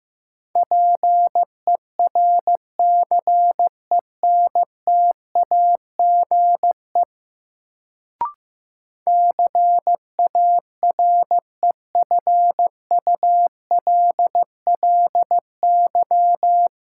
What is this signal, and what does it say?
Morse code, heard as a keyed tone, about 15 wpm, 700 Hz. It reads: PERCENTAGE CAREFULLY
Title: Sentences from Top 1300 Words - New Word Drill Alt - 15wpm